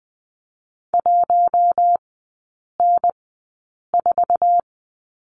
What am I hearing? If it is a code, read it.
Morse code, heard as a keyed tone, about 20 wpm, 700 Hz. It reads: 1 N 4
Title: 047 Lesson 9e - Number 4 - Three Characters 20wpm